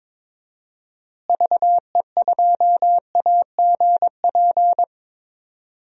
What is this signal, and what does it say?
Morse code, heard as a keyed tone, about 21 wpm, 700 Hz. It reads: VE2AGP
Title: DXCC-100 Call Signs - 6 Characters - Repeated 2x - 22wpm